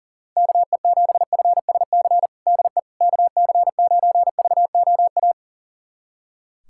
Morse code, heard as a keyed tone, about 37 wpm, 700 Hz. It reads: KE7FSC DE KC9VOA
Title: USA Callsign DE Callsign - 6 Characters - 40wpm